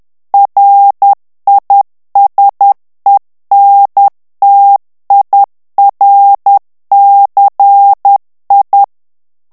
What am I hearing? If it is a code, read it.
Morse code, heard as a keyed tone, about 10 wpm, 800 Hz. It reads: RISENTIRCI
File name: morse mp3 / ITALIEN texte clair 1 vitesse 10 mots minute